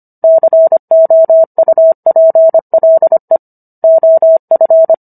Morse code, heard as a keyed tone, about 25 wpm, 650 Hz. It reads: COUPLE OF